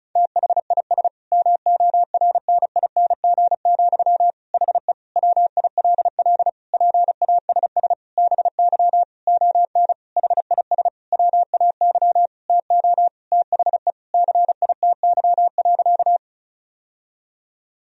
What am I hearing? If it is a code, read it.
Morse code, heard as a keyed tone, about 34 wpm, 700 Hz. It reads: THIS MORNING, HE WILL PASS BY ON HIS WAY TO THE CITY.
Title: Sentences from Top 1000 Words - Review Part 1 - 35wpm